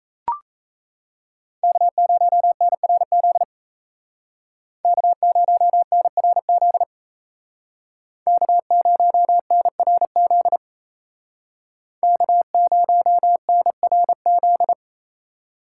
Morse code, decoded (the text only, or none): K0NRZ K0NRZ K0NRZ K0NRZ